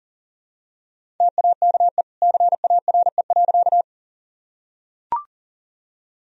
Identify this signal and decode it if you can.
Morse code, heard as a keyed tone, about 40 wpm, 700 Hz. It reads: TAKE CARE.
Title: Sentences from Top 1000 Words - Review Part 1 - 40wpm